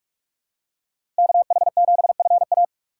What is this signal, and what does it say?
Morse code, heard as a keyed tone, about 46 wpm, 700 Hz. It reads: KH7FA
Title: USA Call Signs - 5 Characters - 2x Character Spacing - 45wpm